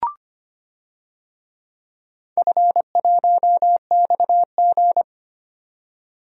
Morse code, heard as a keyed tone, about 25 wpm, 700 Hz. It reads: F1XG
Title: DXCC-100 Call Signs - 4 Characters - 25wpm